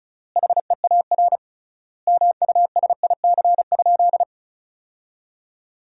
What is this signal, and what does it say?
Morse code, heard as a keyed tone, about 36 wpm, 700 Hz. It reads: HEAR MUSIC?
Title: 3x Word-Spacing - Sentences from Top 300 Words - 35wpm